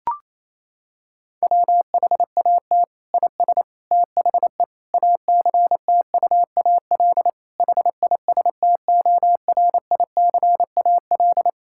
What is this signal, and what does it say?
Morse code, heard as a keyed tone, about 27 wpm, 700 Hz. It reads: WHAT IS THE ACTUAL HISTORICAL
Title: Sentences from Top 1000 Words - Review Part 5 - 28wpm